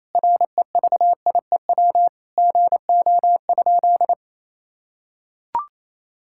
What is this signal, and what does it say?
Morse code, heard as a keyed tone, about 28 wpm, 700 Hz. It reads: REVIEW GO?
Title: Sentences from Top 1000 Words - Review Part 6 - 28wpm